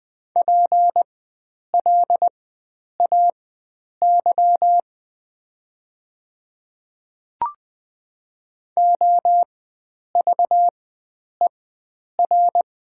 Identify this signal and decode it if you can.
Morse code, heard as a keyed tone, about 20 wpm, spaced as for 9 wpm, 700 Hz. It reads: PLAY OVER
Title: 4x Character Spacing - Top 300 Words - 20wpm